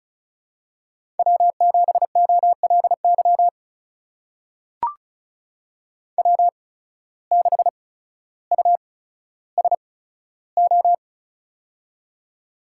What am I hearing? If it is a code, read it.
Morse code, heard as a keyed tone, about 34 wpm, spaced as for 15 wpm, 700 Hz. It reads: W7OLY W6USO
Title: USA Call Signs - 5 Characters - 8x Character Spacing - 35wpm